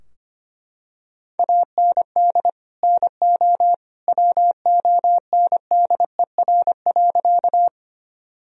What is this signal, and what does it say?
Morse code, heard as a keyed tone, about 25 wpm, 700 Hz. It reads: AND NO WONDER.
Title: Sentences from Top 1000 Words - Review Part 3 - 25wpm